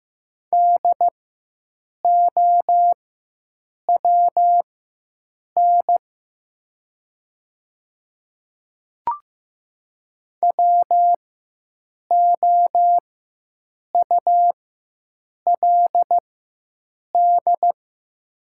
Morse code, heard as a keyed tone, about 15 wpm, spaced as for 7 wpm, 700 Hz. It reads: DOWN WOULD
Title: 4x Character Spacing - Top 200 Words - 15wpm